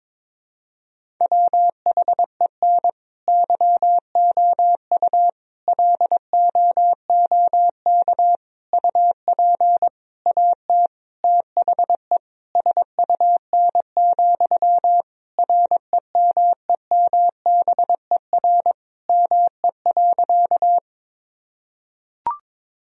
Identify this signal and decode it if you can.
Morse code, heard as a keyed tone, about 22 wpm, 700 Hz. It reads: WHEN YOU LOOK UP AT THE SUN, REMEMBER ME.
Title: Sentences from Top 1000 Words - Review Part 1 - 22wpm